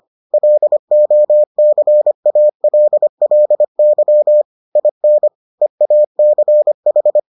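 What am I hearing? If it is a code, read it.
Morse code, heard as a keyed tone, about 25 wpm, 600 Hz. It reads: LOCALLY IN EACH
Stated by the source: Morse edition of the Foundations of Amateur Radio episodes.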